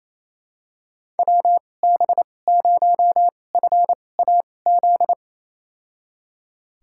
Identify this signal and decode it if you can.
Morse code, heard as a keyed tone, about 28 wpm, spaced as for 20 wpm, 700 Hz. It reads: WB0FAZ
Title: USA Call Signs - 6 Characters - 2x Character Spacing - 28wpm